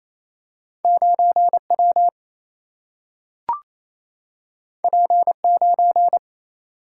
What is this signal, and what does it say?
Morse code, heard as a keyed tone, about 28 wpm, 700 Hz. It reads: 9 W P 9
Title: Single-Letter Single-Number 28wpm